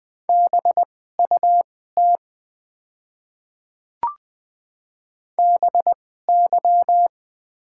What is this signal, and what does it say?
Morse code, heard as a keyed tone, about 21 wpm, spaced as for 15 wpm, 700 Hz. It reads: BUT BY
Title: 2x Character Spacing - Top 100 Words - 20wpm